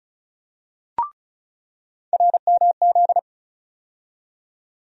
Morse code, heard as a keyed tone, about 36 wpm, 700 Hz. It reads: RMZ
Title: USA Call Sign Postfixes - 3 Letters 35wpm